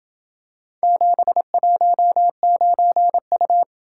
Morse code, heard as a keyed tone, about 26 wpm, 700 Hz. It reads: 719U